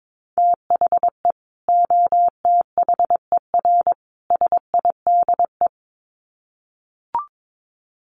Morse code, heard as a keyed tone, about 22 wpm, 700 Hz. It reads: THE OTHER SIDE
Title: Sets of 3 Words 22wpm